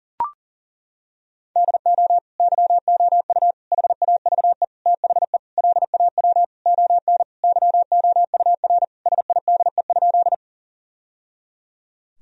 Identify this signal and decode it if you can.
Morse code, heard as a keyed tone, about 40 wpm, 700 Hz. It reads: DO YOU HAVE THE LAW ON YOUR SIDE?
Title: Sentences from Top 1000 Words - Review Part 3 - 40wpm